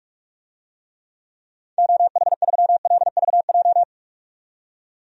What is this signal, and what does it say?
Morse code, heard as a keyed tone, about 46 wpm, 700 Hz. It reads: OH3LVJ
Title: DXCC-100 Call Signs - 6 Characters - 45wpm